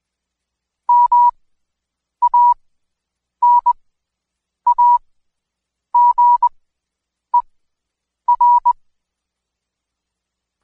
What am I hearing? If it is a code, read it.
Morse code, heard as a keyed tone, about 22 wpm, spaced as for 8 wpm, 970 Hz. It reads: MANAGER
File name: Code Quick / AudioDownld2 / Seg114-Day19-DayDrill 19